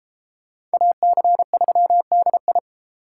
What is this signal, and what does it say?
Morse code, heard as a keyed tone, about 32 wpm, 700 Hz. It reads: AC3DI